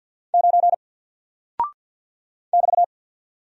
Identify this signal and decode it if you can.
Morse code, heard as a keyed tone, about 48 wpm, 700 Hz. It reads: 9 <BT>
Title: Single Character (Letter:Number:Punctuation:Prosign) 50wpm